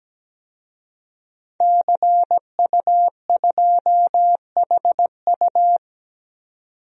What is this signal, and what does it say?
Morse code, heard as a keyed tone, about 17 wpm, 700 Hz. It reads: CU2HU
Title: DXCC-100 Call Signs - 5 Characters - 17wpm